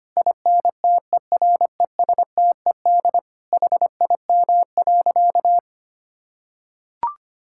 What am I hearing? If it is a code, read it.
Morse code, heard as a keyed tone, about 25 wpm, 700 Hz. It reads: INTERESTED HIM.
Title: Sentences from Top 1000 Words - Review Part 6 - 25wpm